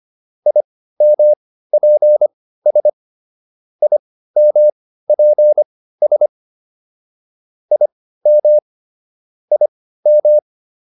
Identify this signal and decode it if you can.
Morse code, heard as a keyed tone, about 25 wpm, spaced as for 15 wpm, 600 Hz. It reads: IMPS IMPS IM IM
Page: intermediate-practice-files